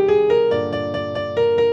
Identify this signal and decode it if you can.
There is no code.